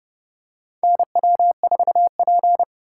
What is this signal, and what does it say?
Morse code, heard as a keyed tone, about 30 wpm, 700 Hz. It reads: NW4P
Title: USA Call Signs - 4 Characters - 2x Character Spacing - 30wpm